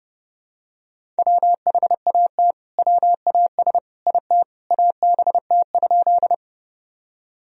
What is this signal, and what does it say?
Morse code, heard as a keyed tone, about 30 wpm, 700 Hz. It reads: WHAT WAS IT ABT?
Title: Sentences from Top 100 Words and Abbreviations 30wpm